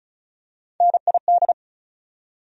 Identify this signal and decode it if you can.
Morse code, heard as a keyed tone, about 36 wpm, 700 Hz. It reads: NID
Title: USA Call Sign Postfixes - 3 Letters 35wpm